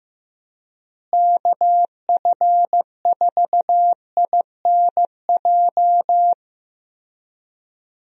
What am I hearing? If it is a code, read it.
Morse code, heard as a keyed tone, about 15 wpm, 700 Hz. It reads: KF4INJ
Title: USA Call Signs - 6 Characters - Repeated 3x - 15wpm